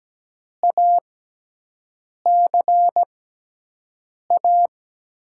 Morse code, heard as a keyed tone, about 17 wpm, spaced as for 6 wpm, 700 Hz. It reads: ACA
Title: USA Call Signs - 6x Character Spacing - 17wpm